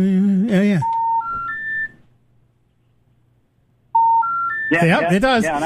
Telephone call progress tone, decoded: special information tone